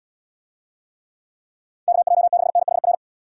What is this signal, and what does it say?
Morse code, heard as a keyed tone, about 78 wpm, 700 Hz. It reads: KJ6NHR